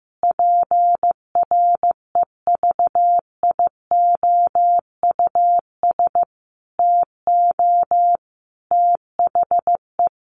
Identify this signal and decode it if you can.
Morse code, heard as a keyed tone, about 15 wpm, 700 Hz. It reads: PREVIOUS TO THE
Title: Sentences from Top 1500 Words 15wpm